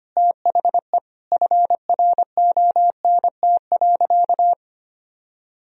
Morse code, heard as a keyed tone, about 25 wpm, 700 Hz. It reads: THE FRONT.
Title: Sentences from Top 1000 Words - Review Part 2 - 25wpm